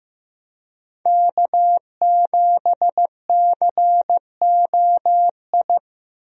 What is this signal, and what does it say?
Morse code, heard as a keyed tone, about 15 wpm, 700 Hz. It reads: K7COI
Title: USA Call Signs - 5 Characters - 15wpm